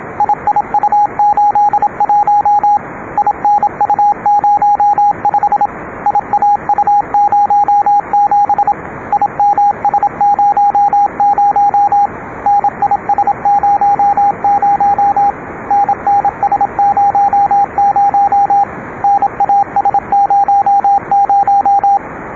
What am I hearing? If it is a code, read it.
Morse code, heard as a keyed tone, about 27 wpm, 780 Hz, with radio noise. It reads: IIU81 INU05 IAU07 IMS00 NIS00 NNS00 NAS00